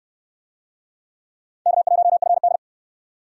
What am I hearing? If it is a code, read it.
Morse code, heard as a keyed tone, about 62 wpm, 700 Hz. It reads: K1FD